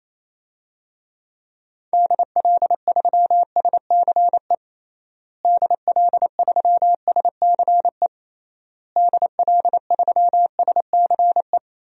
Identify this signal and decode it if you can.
Morse code, heard as a keyed tone, about 28 wpm, 700 Hz. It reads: DL3SCE DL3SCE DL3SCE
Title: DXCC-100 Call Signs - 6 Characters - Repeated 3x - 28wpm